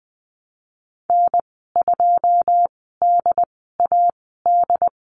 Morse code, heard as a keyed tone, about 20 wpm, spaced as for 15 wpm, 700 Hz. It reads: N2DAD
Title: 082 Lesson 14g - Number 2 - Call Signs 20wpm